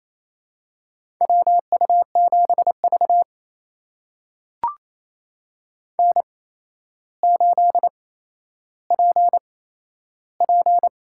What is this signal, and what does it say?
Morse code, heard as a keyed tone, about 28 wpm, spaced as for 12 wpm, 700 Hz. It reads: WU7V N8PP
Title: USA Call Signs - 4 Characters - 8x Character Spacing - 28wpm